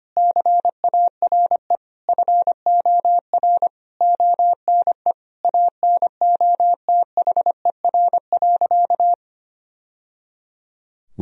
Morse code, heard as a keyed tone, about 25 wpm, 700 Hz. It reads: CARE FOR ONE ANOTHER.
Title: Sentences from Top 1000 Words - Review Part 1 - 25wpm